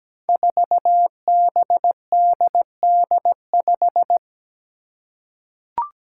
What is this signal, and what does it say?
Morse code, heard as a keyed tone, about 17 wpm, 700 Hz. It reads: 4BDD5
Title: Five Challenging Letter-Number Combinations 17wpm